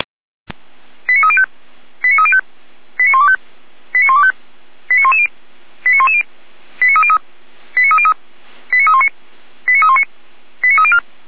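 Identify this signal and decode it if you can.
Five-tone selective call (ZVEI), heard as quick runs of five tones, tones of about 70 ms. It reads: 89305 89305 89125 89125 89110 89110 89303 89303 89319 89319 89304
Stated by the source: ZVEI five-tone selective call received on SDR, from audio.f85360.feuerwehr-orlamuende.org (Feuerwehr Orlamünde fire service).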